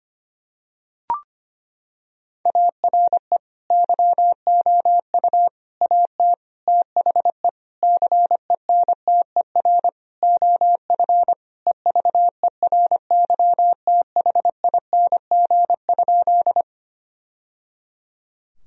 Morse code, heard as a keyed tone, about 25 wpm, 700 Hz. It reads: ARE YOU AT THE CENTER OF EVERYTHING?
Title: Sentences from Top 1000 Words - Review Part 2 - 25wpm